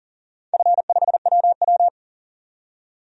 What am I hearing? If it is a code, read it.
Morse code, heard as a keyed tone, about 40 wpm, 700 Hz. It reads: F5WW
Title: DXCC-100 Call Signs - 4 Characters - 40wpm